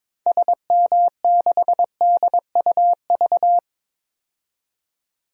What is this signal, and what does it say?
Morse code, heard as a keyed tone, about 22 wpm, 700 Hz. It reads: SM6DUV
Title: DXCC-100 Call Signs - 6 Characters - 22wpm